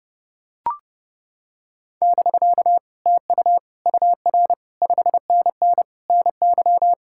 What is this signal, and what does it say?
Morse code, heard as a keyed tone, about 29 wpm, 700 Hz. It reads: <BK> TU UR 5NN NY